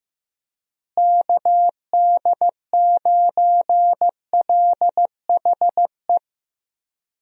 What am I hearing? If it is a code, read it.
Morse code, heard as a keyed tone, about 16 wpm, 700 Hz. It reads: KD9LHE
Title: USA Call Signs - 6 Characters - Repeated 2x - 15wpm